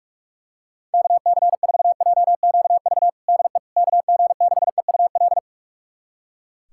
Morse code, heard as a keyed tone, about 44 wpm, 700 Hz. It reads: KC4JQV DE KG6EUL
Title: USA Callsign DE Callsign - 6 Characters - 45wpm